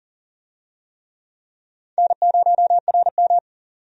Morse code, heard as a keyed tone, about 40 wpm, 700 Hz. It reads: N0RM
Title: USA Call Signs - 4 Characters - 40wpm